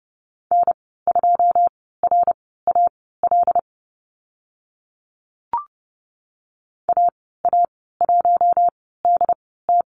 Morse code, heard as a keyed tone, about 30 wpm, spaced as for 17 wpm, 700 Hz. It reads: N2RAL AA1DT
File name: Lesson 15g - Number 5 - Call Signs 30wpm